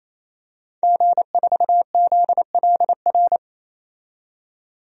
Morse code, heard as a keyed tone, about 28 wpm, 700 Hz. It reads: G4ZLR
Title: DXCC-100 Call Signs - 5 Characters - 28wpm